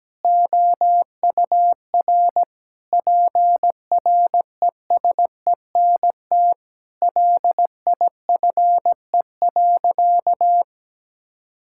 Morse code, heard as a keyed tone, about 17 wpm, 700 Hz. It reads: OUR PRESENT LIFE.
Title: Sentences from Top 1000 Words - Review Part 6 - 17wpm